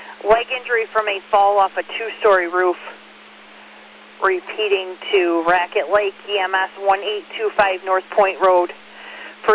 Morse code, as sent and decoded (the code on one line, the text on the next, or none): none